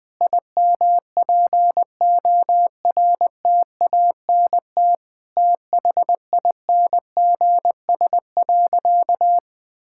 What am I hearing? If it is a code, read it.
Morse code, heard as a keyed tone, about 20 wpm, 700 Hz. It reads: IMPORTANT THINGS.
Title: Sentences with 4-Word Phrases - 20wpm